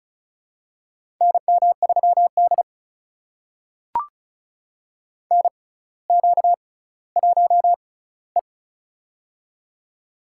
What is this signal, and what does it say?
Morse code, heard as a keyed tone, about 36 wpm, spaced as for 18 wpm, 700 Hz. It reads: NM3D NQ1E
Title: USA Call Signs - 4 Characters - 6x Character Spacing - 35wpm